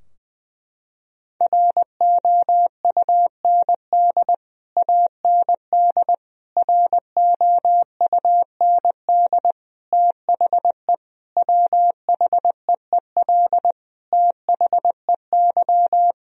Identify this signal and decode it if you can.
Morse code, heard as a keyed tone, about 20 wpm, 700 Hz. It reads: ROUND AND ROUND THE WHEEL THEY
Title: Sentences from Top 1000 Words - Review Part 3 - 20wpm